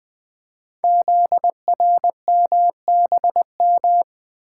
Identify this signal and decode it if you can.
Morse code, heard as a keyed tone, about 20 wpm, 700 Hz. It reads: ZRMBM
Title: Code Groups - 5 Letters - 20wpm